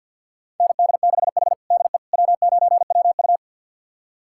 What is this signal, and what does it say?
Morse code, heard as a keyed tone, about 51 wpm, 700 Hz. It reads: ND6H DE W9WU
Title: USA Callsign DE Callsign - 4 Characters - 50wpm